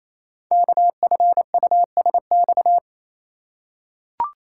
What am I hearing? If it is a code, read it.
Morse code, heard as a keyed tone, about 28 wpm, 700 Hz. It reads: KFUSX
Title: Code Groups - 5 Letters - 28wpm